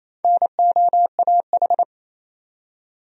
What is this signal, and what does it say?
Morse code, heard as a keyed tone, about 27 wpm, 700 Hz. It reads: NOAH
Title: USA Common Names - Max 4 Letters - 28wpm